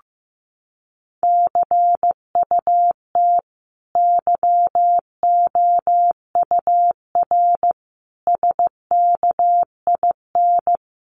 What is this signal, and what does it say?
Morse code, heard as a keyed tone, about 15 wpm, 700 Hz. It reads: CUT YOUR SKIN
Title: Sentences from Top 1300 Words 15wpm